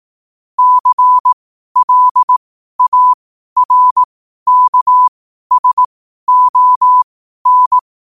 Morse code, heard as a keyed tone, about 18 wpm, spaced as for 12 wpm, 1000 Hz. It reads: CLARKSON
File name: qotdfiles / 12wpm